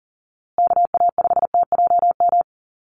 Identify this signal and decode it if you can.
Morse code, heard as a keyed tone, about 40 wpm, 700 Hz. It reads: KA5TJM
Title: USA Call Signs Repeated 3x 40wpm